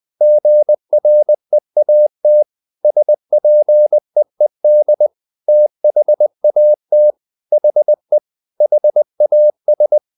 Morse code, heard as a keyed tone, about 20 wpm, 600 Hz. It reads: GREAT SPEED THAT HE HAS
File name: the old man and the sea part 2 20wpm